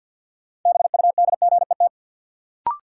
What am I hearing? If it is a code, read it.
Morse code, heard as a keyed tone, about 53 wpm, 700 Hz. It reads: BUDGET
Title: Sentences from Top 1200 Words - New Word Drill - 50wpm